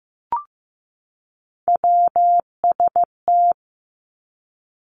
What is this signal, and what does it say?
Morse code, heard as a keyed tone, about 15 wpm, 700 Hz. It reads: WST
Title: Three Challenging Letter-Number Combinations 15wpm